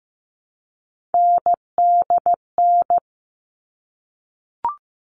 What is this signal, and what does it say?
Morse code, heard as a keyed tone, about 15 wpm, 700 Hz. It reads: NDN
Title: All Cut Numbers - 3-Digit 15wpm